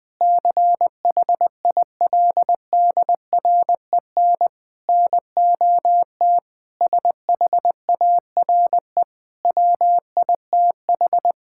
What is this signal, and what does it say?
Morse code, heard as a keyed tone, about 20 wpm, 700 Hz. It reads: CHILDREN NOT SHARE WITH
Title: Sentences from Top 1000 Words - Review Part 6 - 20wpm